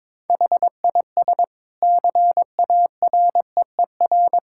Morse code, heard as a keyed tone, about 22 wpm, 700 Hz. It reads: HIS CAREER
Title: Sentences from Top 1000 Words - Review Part 6 - 22wpm